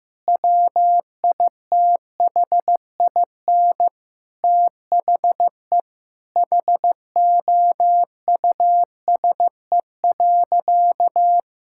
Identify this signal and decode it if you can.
Morse code, heard as a keyed tone, about 16 wpm, 700 Hz. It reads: WITHIN THE HOUSE.
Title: Sentences from Top 1000 Words - Review Part 6 - 15wpm